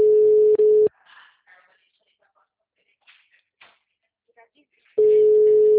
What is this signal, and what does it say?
Ringback tone (Europe, 425 Hz), with speech.
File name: out- -2025- - - .5674